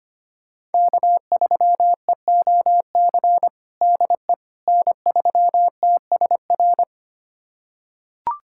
Morse code, heard as a keyed tone, about 25 wpm, 700 Hz. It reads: K3EOC DE N3TSR
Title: USA Callsign DE Callsign - 5 Characters - 25wpm